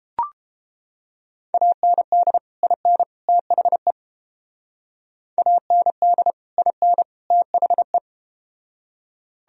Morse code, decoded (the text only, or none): AND IN THE AND IN THE